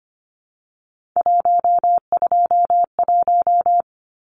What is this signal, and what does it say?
Morse code, heard as a keyed tone, about 25 wpm, 700 Hz. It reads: 121
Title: Numbers - 3-Digit 25wpm